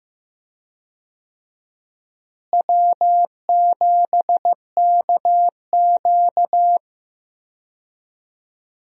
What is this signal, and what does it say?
Morse code, heard as a keyed tone, about 15 wpm, 700 Hz. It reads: W7KQ
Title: USA Call Signs - 4 Characters - Repeated 3x - 15wpm